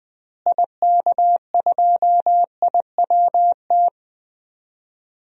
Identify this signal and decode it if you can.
Morse code, heard as a keyed tone, about 20 wpm, 700 Hz. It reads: IK2IWT
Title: DXCC-100 Call Signs - 6 Characters - Repeated 3x - 20wpm